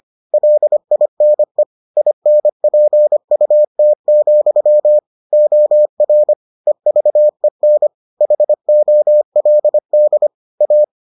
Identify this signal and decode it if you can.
Morse code, heard as a keyed tone, about 25 wpm, 600 Hz. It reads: LINE INPUT, OR EVEN HOLD A